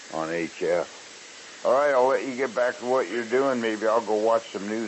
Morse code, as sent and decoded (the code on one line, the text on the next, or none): none